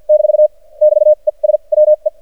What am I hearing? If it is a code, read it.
Morse code, heard as a keyed tone, about 47 wpm, spaced as for 37 wpm, 610 Hz. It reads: - <BT> ES WE
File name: archived / CRSnet